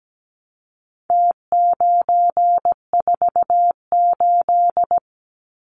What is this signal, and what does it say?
Morse code, heard as a keyed tone, about 17 wpm, 700 Hz. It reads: T948
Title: Numbers - Contest Style - 4-Digit 17wpm